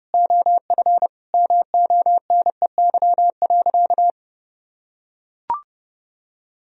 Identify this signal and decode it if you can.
Morse code, heard as a keyed tone, about 30 wpm, 700 Hz. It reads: OF MONEY.
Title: Sentences from Top 1000 Words - Review Part 4 - 30wpm